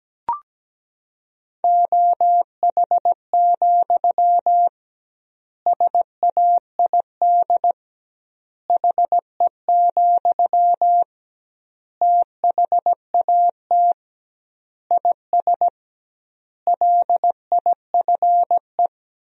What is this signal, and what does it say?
Morse code, heard as a keyed tone, about 17 wpm, 700 Hz. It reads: OH, SAID HE, THAT IS LIFE
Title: 2x Word-Spacing - Sentences from Top 500 Words - 17wpm